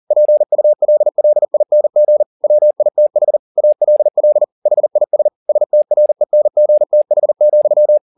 Morse code, heard as a keyed tone, about 40 wpm, 600 Hz. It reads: PULLING WITH ALL HIS STRENGTH,
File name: the old man and the sea 2 40wpm